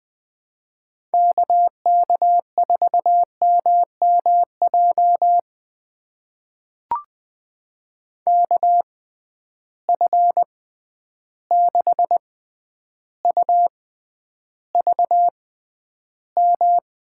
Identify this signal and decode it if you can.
Morse code, heard as a keyed tone, about 20 wpm, spaced as for 10 wpm, 700 Hz. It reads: KK4MMJ KF6UVM
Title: USA Call Signs - 6 Characters - 6x Character Spacing - 20wpm